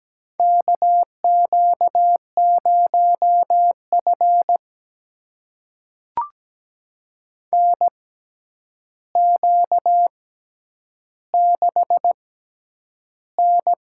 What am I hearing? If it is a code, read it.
Morse code, heard as a keyed tone, about 17 wpm, spaced as for 9 wpm, 700 Hz. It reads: KQ0F NQ6N
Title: USA Call Signs - 4 Characters - 6x Character Spacing - 17wpm